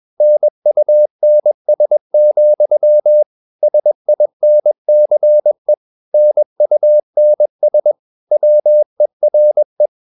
Morse code, heard as a keyed tone, about 21 wpm, 600 Hz. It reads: NUNS, SINCE NUNS WERE
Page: CW med Gnister